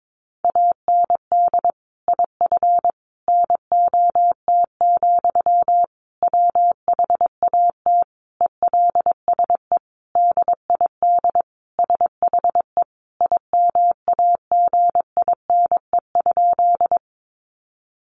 Morse code, heard as a keyed tone, about 22 wpm, 700 Hz. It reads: AND IF NOT, WHAT ELSE DID SHE IMAGINE?
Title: Sentences from Top 1400 Words 22wpm